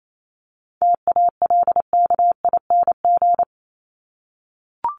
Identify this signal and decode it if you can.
Morse code, heard as a keyed tone, about 28 wpm, 700 Hz. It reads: TALKING
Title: ING Suffix - Top 100 Words with Base Words 28wpm